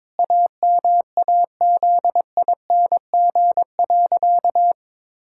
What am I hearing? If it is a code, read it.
Morse code, heard as a keyed tone, about 22 wpm, 700 Hz. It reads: AMAZING.
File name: practice / 22wpm / Sentences with 3-Word Phrases - 22wpm